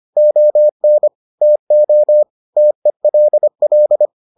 Morse code, heard as a keyed tone, about 25 wpm, 600 Hz. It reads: ON TO TELL